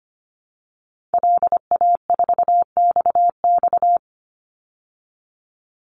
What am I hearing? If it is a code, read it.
Morse code, heard as a keyed tone, about 25 wpm, 700 Hz. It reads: LA4XX
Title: Call Signs Repeated 2x - 25wpm